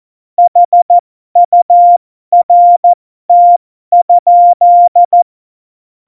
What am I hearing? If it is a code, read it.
Morse code, heard as a keyed tone, about 14 wpm, spaced as for 12 wpm, 700 Hz. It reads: HURT?
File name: Morsekurs / Witze